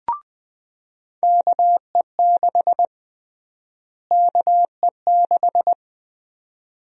Morse code, heard as a keyed tone, about 20 wpm, 700 Hz. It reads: KE6 KE6
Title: USA Call Sign Prefixes - Repeated 2x - 20wpm